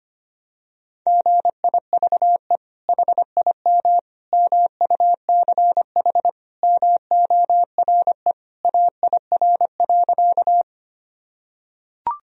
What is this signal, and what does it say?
Morse code, heard as a keyed tone, about 25 wpm, 700 Hz. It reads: GIVE HIM MUCH MORE AIR.
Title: Sentences from Top 1000 Words - Review Part 1 - 25wpm